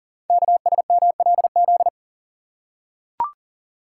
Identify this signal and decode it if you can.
Morse code, heard as a keyed tone, about 39 wpm, 690 Hz. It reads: KSMLZ
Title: Code Groups - 5 Letters - 40wpm